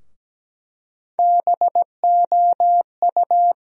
Morse code, heard as a keyed tone, about 17 wpm, 700 Hz. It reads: BOU
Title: USA Call Sign Postfixes - 3 Letters 17wpm